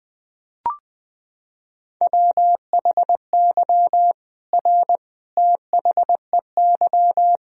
Morse code, heard as a keyed tone, about 21 wpm, 700 Hz. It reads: WHY R THEY
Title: Sentences from Top 200 Words and Abbreviations 20wpm